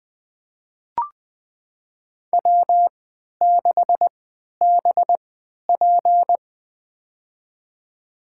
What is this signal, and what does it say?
Morse code, heard as a keyed tone, about 20 wpm, spaced as for 11 wpm, 700 Hz. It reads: W6BP
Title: USA Call Signs - 4 Characters - 3x Character Spacing - 20wpm